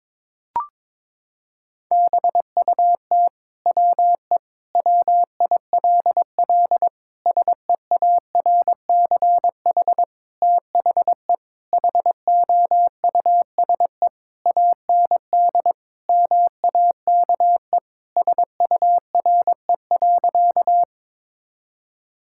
Morse code, heard as a keyed tone, about 22 wpm, 700 Hz. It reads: BUT WE WILL SEARCH THE HOUSE AND MAKE SURE.
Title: Sentences from Top 1000 Words - Review Part 5 - 22wpm